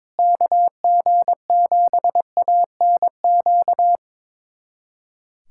Morse code, heard as a keyed tone, about 21 wpm, 700 Hz. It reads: KG7ANQ
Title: USA Callsign DE Callsign - 6 Characters - 22wpm